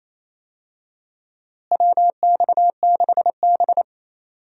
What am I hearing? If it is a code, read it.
Morse code, heard as a keyed tone, about 28 wpm, 700 Hz. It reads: WX6B
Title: USA Call Signs - 4 Characters - Repeated 2x - 28wpm